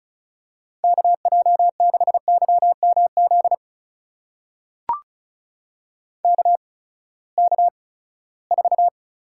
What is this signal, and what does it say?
Morse code, heard as a keyed tone, about 36 wpm, spaced as for 20 wpm, 700 Hz. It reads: KJ6YMZ KK4
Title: USA Call Signs - 6 Characters - 8x Character Spacing - 35wpm